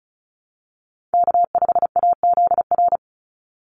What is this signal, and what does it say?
Morse code, heard as a keyed tone, about 34 wpm, 700 Hz. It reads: K5AZR
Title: Call Signs - 35wpm